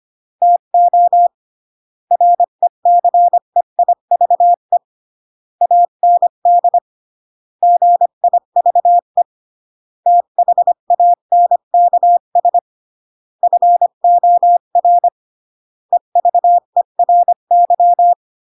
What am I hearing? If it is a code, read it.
Morse code, heard as a keyed tone, about 25 wpm, 700 Hz. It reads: TO RECEIVE AND GIVE THANKS FOR EVERY